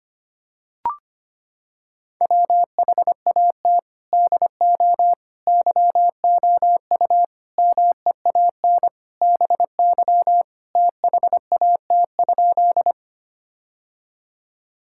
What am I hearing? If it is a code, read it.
Morse code, heard as a keyed tone, about 25 wpm, 700 Hz. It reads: WHAT DO YOU MEAN BY THAT?
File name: practice / 25wpm / Sentences with 4-Word Phrases - 25wpm